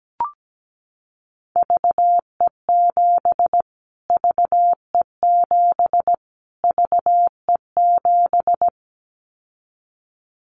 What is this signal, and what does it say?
Morse code, heard as a keyed tone, about 17 wpm, 700 Hz. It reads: VE7 VE7 VE7
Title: Callsign Prefixes for Top 25 DXCC - Repeated 3x - 17wpm